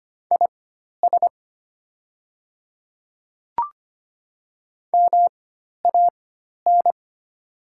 Morse code, heard as a keyed tone, about 25 wpm, spaced as for 12 wpm, 700 Hz. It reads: IS MAN
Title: 4x Character Spacing - Top 200 Words - 25wpm